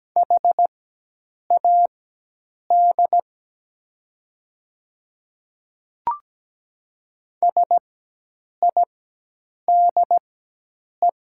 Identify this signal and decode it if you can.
Morse code, heard as a keyed tone, about 17 wpm, spaced as for 8 wpm, 700 Hz. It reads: HAD SIDE
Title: 4x Character Spacing - Top 100 Words - 17wpm